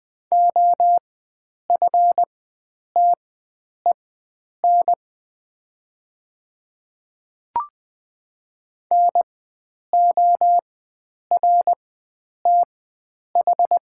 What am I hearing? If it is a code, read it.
Morse code, heard as a keyed tone, about 20 wpm, spaced as for 9 wpm, 700 Hz. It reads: OFTEN NORTH
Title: 4x Character Spacing - Top 300 Words - 20wpm